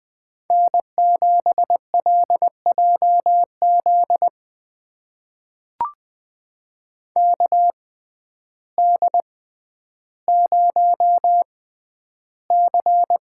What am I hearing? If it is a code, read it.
Morse code, heard as a keyed tone, about 21 wpm, spaced as for 11 wpm, 700 Hz. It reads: N7LJZ KD0C
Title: USA Call Signs - 6x Character Spacing - 20wpm